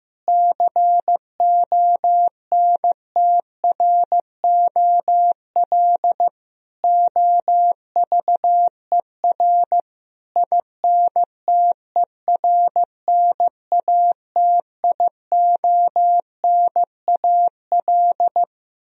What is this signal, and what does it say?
Morse code, heard as a keyed tone, about 16 wpm, 700 Hz. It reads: CONTROL OVER INTERNATIONAL
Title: Sentences from Top 1000 Words - Review Part 4 - 15wpm